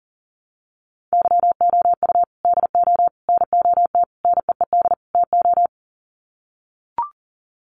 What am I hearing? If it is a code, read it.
Morse code, heard as a keyed tone, about 40 wpm, 700 Hz. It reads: YOU DO NOT NEED TO
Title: Sets of 5 Words 40wpm